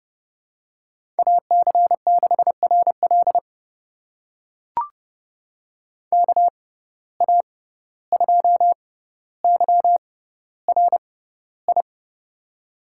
Morse code, heard as a keyed tone, about 31 wpm, spaced as for 15 wpm, 700 Hz. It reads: AC6RL KA2YRI
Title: USA Call Signs - 6x Character Spacing - 30wpm